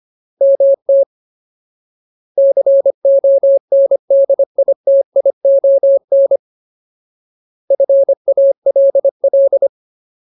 Morse code, heard as a keyed tone, about 25 wpm, 550 Hz. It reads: MT CONDITION FALL
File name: Ultimate Word List / 25WPM Ultimate